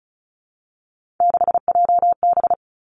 Morse code, heard as a keyed tone, about 36 wpm, 700 Hz. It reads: 6JB
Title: Three Challenging Letter-Number Combinations 35wpm